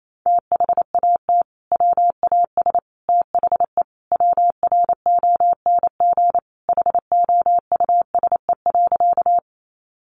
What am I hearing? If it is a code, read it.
Morse code, heard as a keyed tone, about 28 wpm, 700 Hz. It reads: THAT WAS THE WRONG HOUSE.
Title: Sentences from Top 900 Words 28wpm